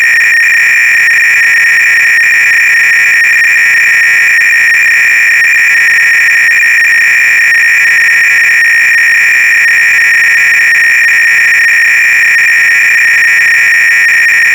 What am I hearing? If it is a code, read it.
300-baud modem data.